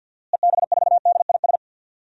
Morse code, heard as a keyed tone, about 48 wpm, 700 Hz. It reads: EB4DIS
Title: DXCC-100 Call Signs - 6 Characters - Repeated 3x - 50wpm